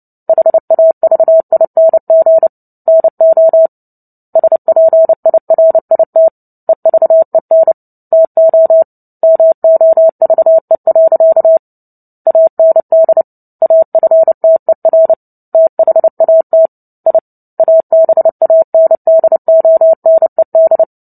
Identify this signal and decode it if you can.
Morse code, heard as a keyed tone, about 30 wpm, 650 Hz. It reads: HAVING NO SPIRIT EVEN TO MOVE. AND AFTER THAT I ABANDONED
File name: Book Two / 03-Chapter 3 - 29 WPM